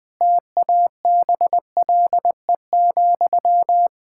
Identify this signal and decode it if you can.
Morse code, heard as a keyed tone, about 20 wpm, 700 Hz. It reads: TABLE,
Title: Sentences from Top 1600 Words 20wpm